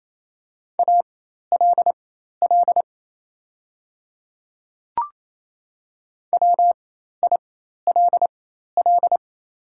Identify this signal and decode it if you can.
Morse code, heard as a keyed tone, about 28 wpm, spaced as for 13 wpm, 700 Hz. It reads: ALL WILL
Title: 4x Character Spacing - Top 100 Words - 28wpm